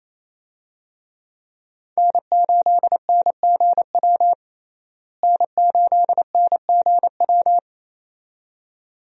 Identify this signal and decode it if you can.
Morse code, heard as a keyed tone, about 28 wpm, 700 Hz. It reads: N8NGW N8NGW
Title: USA Call Signs - 5 Characters - Repeated 2x - 28wpm